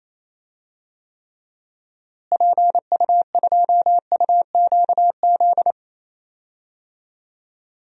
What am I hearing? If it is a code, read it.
Morse code, heard as a keyed tone, about 27 wpm, 700 Hz. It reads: PU2UQZ
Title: DXCC-100 Call Signs - 6 Characters - Repeated 3x - 28wpm